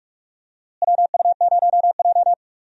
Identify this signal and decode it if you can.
Morse code, heard as a keyed tone, about 46 wpm, 700 Hz. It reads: WU0J
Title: USA Call Signs - 4 Characters - Repeated 3x - 45wpm